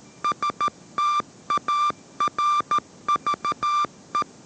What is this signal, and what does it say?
Morse code, heard as a keyed tone, about 13 wpm, 1180 Hz. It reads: STARVE